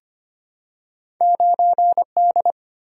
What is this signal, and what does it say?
Morse code, heard as a keyed tone, about 25 wpm, 700 Hz. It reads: 9D
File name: Two Letter-Number Combinations - 25wpm